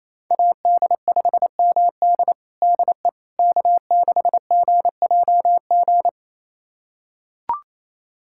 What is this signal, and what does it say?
Morse code, heard as a keyed tone, about 28 wpm, 700 Hz. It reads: AD5MD DE K6GJG
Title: USA Callsign DE Callsign - 5 Characters - 28wpm